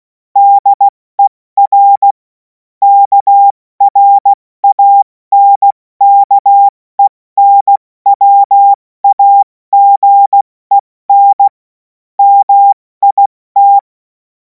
Morse code, heard as a keyed tone, about 16 wpm, 800 Hz. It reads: DER KRANKENWAGEN MIT
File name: Morsekurs / Witze / Witz 07. April 2026 70 F80